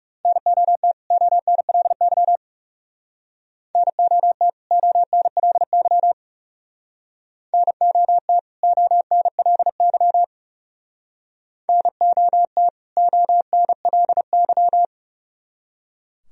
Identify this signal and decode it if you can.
Morse code, heard as a keyed tone, about 36 wpm, 700 Hz. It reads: NOT ONLY NOT ONLY NOT ONLY NOT ONLY
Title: Speed-Racing - Sets of 2 Words 30wpm